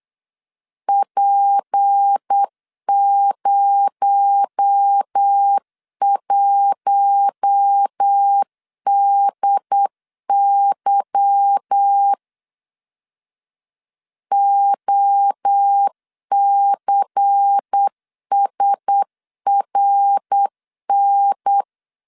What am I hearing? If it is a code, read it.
Morse code, heard as a keyed tone, about 8 wpm, 800 Hz. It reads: P01DY OCSRN
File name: LEZIONI / 8 YJ45 / 8 Y 8wpm 3245